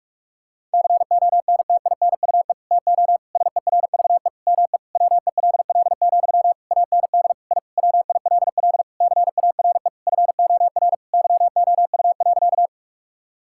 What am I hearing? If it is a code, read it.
Morse code, heard as a keyed tone, about 41 wpm, 700 Hz. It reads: CONTINUE TO SERVE ME WELL, AND I WILL CARE FOR YOU.
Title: Sentences from Top 1000 Words - Review Part 6 - 45wpm